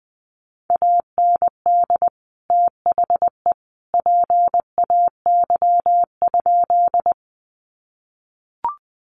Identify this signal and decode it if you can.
Morse code, heard as a keyed tone, about 20 wpm, 700 Hz. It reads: AND THE PAY?
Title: Sentences from Top 700 Words 20wpm